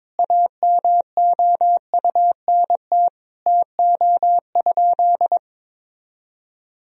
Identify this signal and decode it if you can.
Morse code, heard as a keyed tone, about 21 wpm, 700 Hz. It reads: AMOUNT TO?
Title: Sentences from Top 1000 Words - Review Part 4 - 22wpm